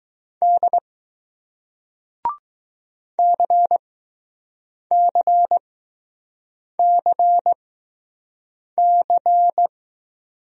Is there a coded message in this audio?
Yes: Morse code from 0.4 to 9.7 s, heard as a keyed tone, about 20 wpm, spaced as for 6 wpm, 700 Hz. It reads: D CCCC